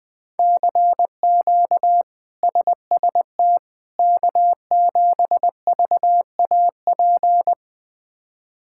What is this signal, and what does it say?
Morse code, heard as a keyed tone, about 20 wpm, 700 Hz. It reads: CQ SST K7VAP
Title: K1USN SST QSO Practice - 20wpm